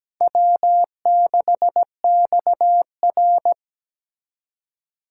Morse code, heard as a keyed tone, about 17 wpm, 700 Hz. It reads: W6XR
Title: K1USN SST QSO Practice - 17wpm